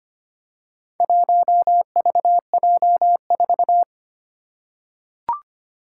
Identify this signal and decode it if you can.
Morse code, heard as a keyed tone, about 25 wpm, 690 Hz. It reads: 1VJ4
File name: practice / 25wpm / Four Letter-Number Combinations - 25wpm